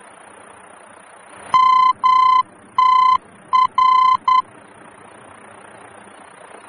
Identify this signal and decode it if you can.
Morse code, heard as a keyed tone, about 10 wpm, 1020 Hz. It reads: MTR
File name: germany / mx / MTR-D-VOR